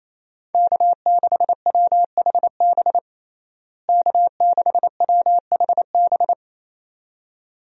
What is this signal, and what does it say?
Morse code, heard as a keyed tone, about 28 wpm, 700 Hz. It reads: K6WHB K6WHB
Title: USA Call Signs - 5 Characters - Repeated 2x - 28wpm